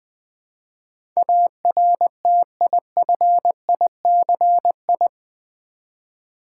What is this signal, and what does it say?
Morse code, heard as a keyed tone, about 20 wpm, 700 Hz. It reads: ARTIFICI
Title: Build A Word - 4 Letters At A Time - 10-Letter Words - 20wpm